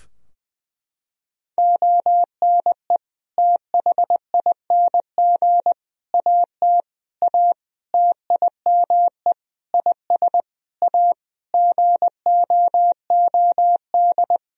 Morse code, heard as a keyed tone, about 20 wpm, 700 Hz. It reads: ONE THING AT A TIME IS A GOOD
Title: Sentences from Top 1000 Words - Review Part 2 - 20wpm